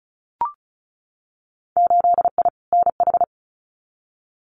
Morse code, heard as a keyed tone, about 34 wpm, 700 Hz. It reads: 8I NH
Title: Winter Field Day Exchanges Encore 35wpm